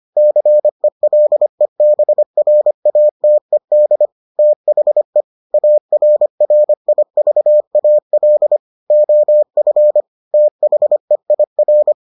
Morse code, heard as a keyed tone, about 25 wpm, 600 Hz. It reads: CELEBRATED THE ARRIVAL OF THEIR